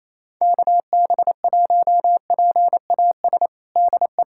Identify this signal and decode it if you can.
Morse code, heard as a keyed tone, about 28 wpm, 700 Hz. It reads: KB1PAS DE
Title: USA Callsign DE Callsign 28wpm